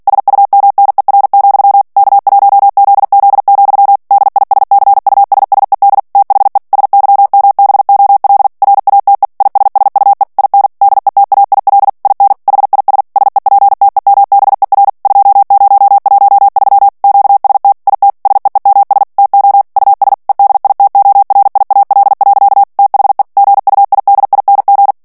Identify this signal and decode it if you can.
Morse code, heard as a keyed tone, about 48 wpm, 800 Hz. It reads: SUMNER, K1ZZ, DISCUSSED THE SYMBOL RATE ISSUE IN DETAIL IN HIS SEPTEMBER 2013 QST IT SEEMS TO US EDITORIAL. THE GUIDING
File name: ANGLAIS texte clair 3 vitesse 45 mots minute